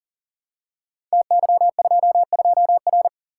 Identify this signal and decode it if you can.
Morse code, heard as a keyed tone, about 39 wpm, 700 Hz. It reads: TY22R
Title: Code Groups - 5 Letter:Numbers - 40wpm